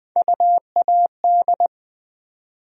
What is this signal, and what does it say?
Morse code, heard as a keyed tone, about 20 wpm, 700 Hz. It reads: UAD
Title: USA Call Sign Postfixes - 3 Letters 20wpm